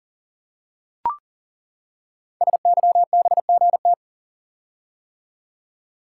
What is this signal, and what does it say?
Morse code, heard as a keyed tone, about 40 wpm, 700 Hz. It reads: SYBGT
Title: Code Groups - 5 Letters - 40wpm